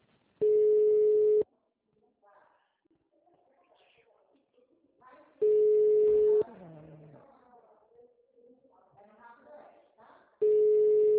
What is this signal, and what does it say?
Ringback tone (Europe, 425 Hz), with speech.
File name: out- -2035- - - .242